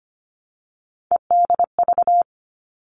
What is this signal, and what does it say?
Morse code, heard as a keyed tone, about 25 wpm, 700 Hz. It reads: EDV
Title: All Cut Numbers - 3-Digit 25wpm